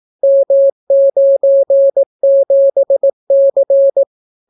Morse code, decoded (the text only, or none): M97C